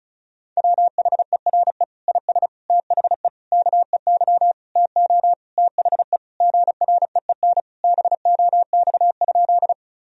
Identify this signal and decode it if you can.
Morse code, heard as a keyed tone, about 34 wpm, 700 Hz. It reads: WHERE IS THE KEY TO THE GREEN BOX?
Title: Sentences from Top 1000 Words - Review Part 4 - 35wpm